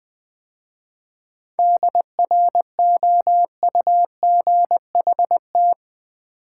Morse code, heard as a keyed tone, about 20 wpm, 700 Hz. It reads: DROUGHT
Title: 7 Letter Words - First 3 Letters Given - 20wpm